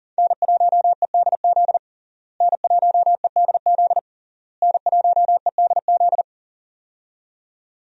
Morse code, heard as a keyed tone, about 40 wpm, 700 Hz. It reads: N1EDZ N1EDZ N1EDZ
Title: USA Call Signs - 5 Characters - Repeated 3x - 40wpm